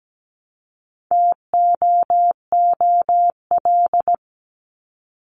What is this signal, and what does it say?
Morse code, heard as a keyed tone, about 17 wpm, 700 Hz. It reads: TOOL
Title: Top 751-1000 Words 17wpm